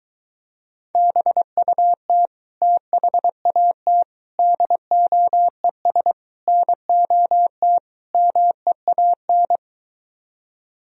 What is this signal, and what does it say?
Morse code, heard as a keyed tone, about 22 wpm, 700 Hz. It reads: BUT THAT DOES NOT MEAN